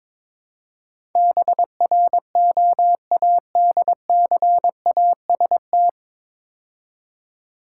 Morse code, heard as a keyed tone, about 22 wpm, 700 Hz. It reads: BROADCAST
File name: 9 Letter Words - First 3 Letters Given - 22wpm